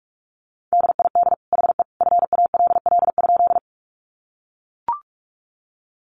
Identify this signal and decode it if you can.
Morse code, heard as a keyed tone, about 44 wpm, 700 Hz. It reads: DID HE FALL?
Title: Sentences from Top 500 Words 45wpm